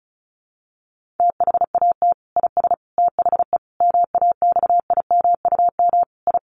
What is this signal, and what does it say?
Morse code, heard as a keyed tone, about 34 wpm, 700 Hz. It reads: THAT IS THE MAXIMUM I
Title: Sentences from Top 1300 Words 35wpm